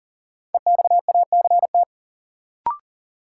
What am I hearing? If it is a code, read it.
Morse code, heard as a keyed tone, about 40 wpm, 700 Hz. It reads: EXACT
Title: Sentences from Top 1300 Words - New Word Drill - 40wpm